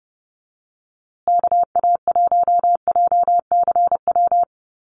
Morse code, heard as a keyed tone, about 30 wpm, 700 Hz. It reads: KA1JCW
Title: USA Call Signs Repeated 2x 30wpm